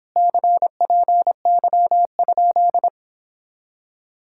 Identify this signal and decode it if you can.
Morse code, heard as a keyed tone, about 26 wpm, 700 Hz. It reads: CPY?